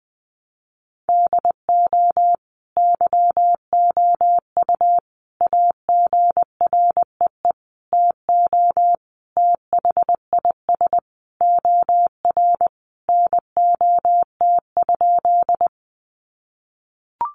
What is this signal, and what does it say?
Morse code, heard as a keyed tone, about 20 wpm, 700 Hz. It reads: DO YOU AGREE TO THIS OR NOT?
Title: Sentences from Top 1100 Words 20wpm